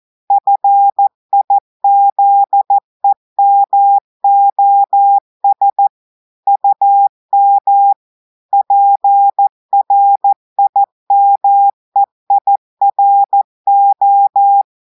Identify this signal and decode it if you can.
Morse code, heard as a keyed tone, about 14 wpm, 800 Hz. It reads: FIZEMOS UM PRIMEIRO
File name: audios / portugais / portugais texte en clair 1 vitesse 14 mots minutes